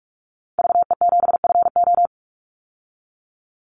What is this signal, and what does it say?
Morse code, heard as a keyed tone, about 44 wpm, 700 Hz. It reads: VE7FO
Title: Call Signs Repeated 3x - 45wpm